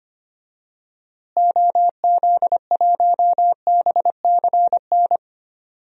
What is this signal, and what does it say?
Morse code, heard as a keyed tone, about 25 wpm, 700 Hz. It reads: OZ1BCN
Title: DXCC-100 Call Signs - 6 Characters - Repeated 3x - 25wpm